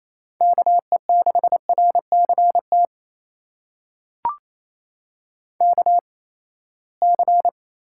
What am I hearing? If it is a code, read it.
Morse code, heard as a keyed tone, about 28 wpm, spaced as for 19 wpm, 700 Hz. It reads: KE6RCT KC
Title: USA Call Signs - 6 Characters - 8x Character Spacing - 28wpm